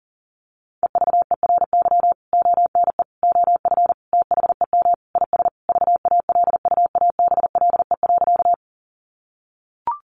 Morse code, heard as a keyed tone, about 40 wpm, 700 Hz. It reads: EVERY ONE OF THEM IS VALUABLE.
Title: Sentences from Top 1100 Words 40wpm